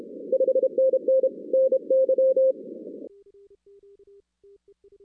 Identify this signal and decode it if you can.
Morse code, heard as a keyed tone, about 29 wpm, 530 Hz. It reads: 5NN NY
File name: 2016 arrl dx cw / 2016-ARRL-DX-CW-K5TR- - 0-46